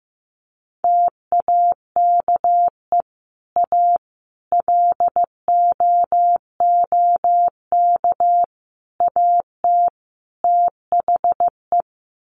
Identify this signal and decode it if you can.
Morse code, heard as a keyed tone, about 15 wpm, 700 Hz. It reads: TAKE A LOOK AT THE
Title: Sets of 5 Words 15wpm